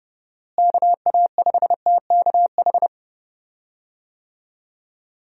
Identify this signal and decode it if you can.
Morse code, heard as a keyed tone, about 30 wpm, 700 Hz. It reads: KA5TKH
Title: USA Call Signs - 6 Characters - Repeated 2x - 30wpm